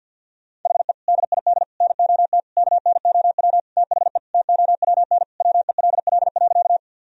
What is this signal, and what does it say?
Morse code, heard as a keyed tone, about 51 wpm, 700 Hz. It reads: HE DID NOT KNOW THE TOWN WELL.
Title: Sentences from Top 1000 Words - Review Part 2 - 50wpm